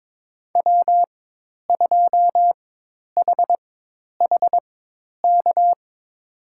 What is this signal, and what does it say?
Morse code, heard as a keyed tone, about 22 wpm, spaced as for 10 wpm, 700 Hz. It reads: W2HHK
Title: USA Call Signs - 5 Characters - 4x Character Spacing - 22wpm